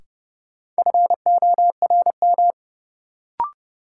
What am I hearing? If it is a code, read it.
Morse code, heard as a keyed tone, about 30 wpm, 700 Hz. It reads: FORM
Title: Sets of 4 Words - New Word Drill - 30wpm